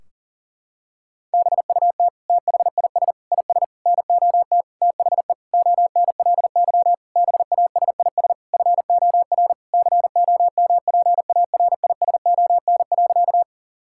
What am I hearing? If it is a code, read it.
Morse code, heard as a keyed tone, about 40 wpm, 700 Hz. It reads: BUT THIS IS NOT THE ONLY BASIS FOR COMPARISON.
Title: Sentences from Top 1600 Words 40wpm